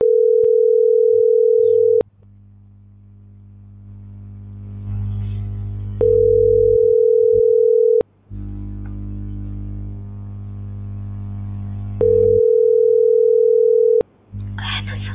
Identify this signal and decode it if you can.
Ringback tone (North America), 440 + 480 Hz.